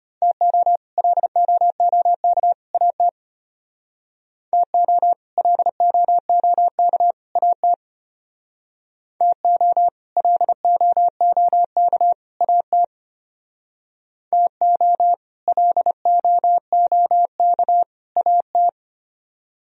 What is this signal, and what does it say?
Morse code, heard as a keyed tone, about 33 wpm, 700 Hz. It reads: TO LOOK AT TO LOOK AT TO LOOK AT TO LOOK AT